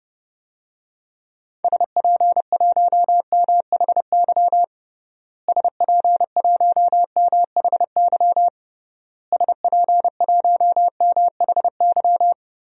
Morse code, heard as a keyed tone, about 32 wpm, 700 Hz. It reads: SP1MHY SP1MHY SP1MHY
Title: DXCC-100 Call Signs - 6 Characters - Repeated 3x - 30wpm